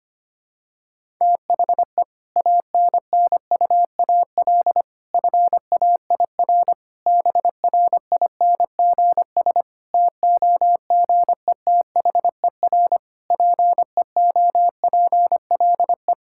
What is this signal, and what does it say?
Morse code, heard as a keyed tone, about 25 wpm, 700 Hz. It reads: THE ANNUAL FAIR BRINGS TOGETHER PEOPLE
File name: practice / 25wpm / Difficult Sentences with Hints - 25wpm